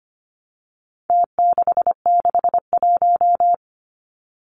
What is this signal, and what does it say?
Morse code, heard as a keyed tone, about 25 wpm, 700 Hz. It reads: T661
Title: Numbers - Contest Style - 4-Digit 25wpm